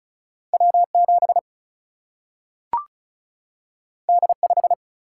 Morse code, heard as a keyed tone, about 36 wpm, 700 Hz. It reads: W 7 D 5
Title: Single-Letter Single-Number 35wpm